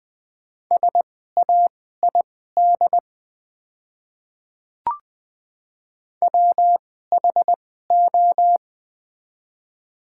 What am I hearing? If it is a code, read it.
Morse code, heard as a keyed tone, about 20 wpm, spaced as for 15 wpm, 690 Hz. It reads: SAID WHO
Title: 2x Character Spacing - Top 100 Words - 20wpm